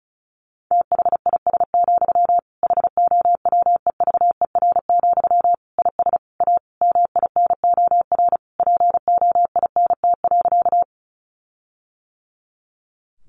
Morse code, heard as a keyed tone, about 34 wpm, 700 Hz. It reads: THIS, HOWEVER, IS A MINOR POINT.
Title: Sentences from Top 1500 Words 35wpm